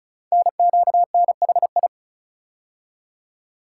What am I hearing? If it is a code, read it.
Morse code, heard as a keyed tone, about 36 wpm, 700 Hz. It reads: NQNHI
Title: Code Groups - 5 Letters - 35wpm